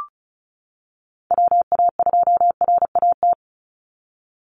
Morse code, heard as a keyed tone, about 34 wpm, 700 Hz. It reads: WA2RAT
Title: USA Call Signs 35wpm